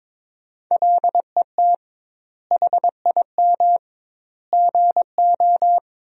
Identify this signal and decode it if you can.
Morse code, heard as a keyed tone, about 22 wpm, 700 Hz. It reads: LET HIM GO
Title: 2x Word-Spacing - Sentences from Top 500 Words - 22wpm